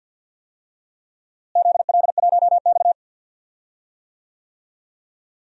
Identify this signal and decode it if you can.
Morse code, heard as a keyed tone, about 51 wpm, 700 Hz. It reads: ZL1X
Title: DXCC-100 Call Signs - 4 Characters - Repeated 2x - 50wpm